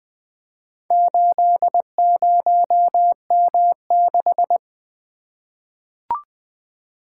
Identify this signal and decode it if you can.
Morse code, heard as a keyed tone, about 20 wpm, 700 Hz. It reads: 80M6
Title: Four Letter-Number Combinations - 20wpm